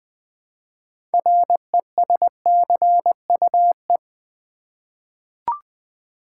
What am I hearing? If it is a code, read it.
Morse code, heard as a keyed tone, about 20 wpm, 700 Hz. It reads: RESCUE
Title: 6 Letter Words - First 3 Letters Given - 20wpm